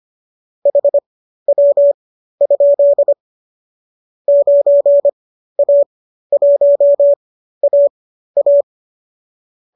Morse code, heard as a keyed tone, about 25 wpm, spaced as for 13 wpm, 570 Hz. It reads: HW? 9A1AA